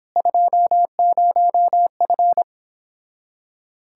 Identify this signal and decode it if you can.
Morse code, heard as a keyed tone, about 26 wpm, 700 Hz. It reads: 20F